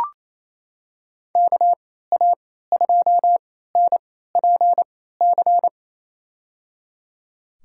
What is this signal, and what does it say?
Morse code, heard as a keyed tone, about 28 wpm, spaced as for 16 wpm, 700 Hz. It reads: KA2NPC
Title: USA Call Signs - 6 Characters - 3x Character Spacing - 28wpm